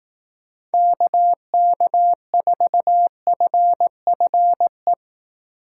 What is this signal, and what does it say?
Morse code, heard as a keyed tone, about 18 wpm, 700 Hz. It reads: KK4FFE
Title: Speed Racing - USA 2x3 Call Signs 15wpm